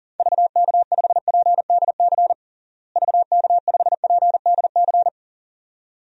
Morse code, heard as a keyed tone, about 40 wpm, 700 Hz. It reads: VK5PDC VK5PDC
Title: DXCC-100 Call Signs - 6 Characters - Repeated 2x - 40wpm